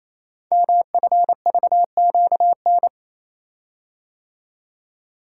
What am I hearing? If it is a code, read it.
Morse code, heard as a keyed tone, about 29 wpm, 700 Hz. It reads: MFVQN
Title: Code Groups - 5 Letters - 28wpm